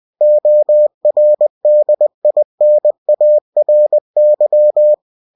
Morse code, heard as a keyed tone, about 20 wpm, 600 Hz. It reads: ORDINARY
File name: cw-books / winnie-the-pooh / winnie-20wpm-page-0099